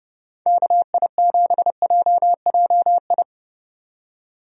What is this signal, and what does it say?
Morse code, heard as a keyed tone, about 30 wpm, 700 Hz. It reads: KI7JJI
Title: USA Call Signs - 6 Characters - 30wpm